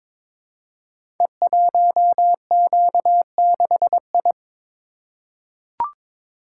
Morse code, heard as a keyed tone, about 22 wpm, 700 Hz. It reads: E1Q6I
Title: Code Groups - 5 Letter:Numbers - 22wpm